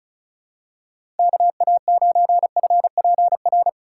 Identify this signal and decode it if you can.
Morse code, heard as a keyed tone, about 36 wpm, 700 Hz. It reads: KA9FPR
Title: USA Call Signs - 6 Characters - 3x Character Spacing - 35wpm